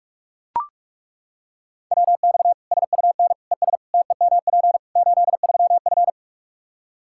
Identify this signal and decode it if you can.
Morse code, heard as a keyed tone, about 41 wpm, 700 Hz. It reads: WX SUN ES TEMP 83F
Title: K1USN SST QSO Practice 45wpm